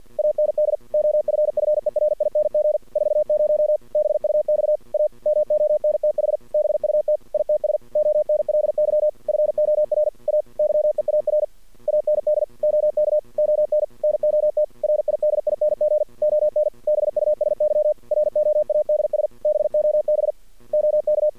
Morse code, heard as a keyed tone, about 48 wpm, 610 Hz. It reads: AND OFFERING 40 BUX A MONTH BUT ITS ONLY FOR A YEAR AND OK ON NOT RIDING ON BUSY ROADS BOB OK
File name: archived / CRSnet